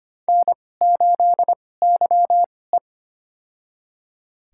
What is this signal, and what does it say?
Morse code, heard as a keyed tone, about 25 wpm, spaced as for 18 wpm, 700 Hz. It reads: N8YE
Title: USA Call Signs - 4 Characters - 2x Character Spacing - 25wpm